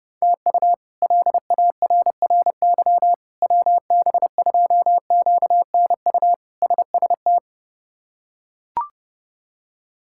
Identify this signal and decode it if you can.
Morse code, heard as a keyed tone, about 30 wpm, 700 Hz. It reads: TU LARRY WB2QNU SST
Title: K1USN SST QSO Practice - 30wpm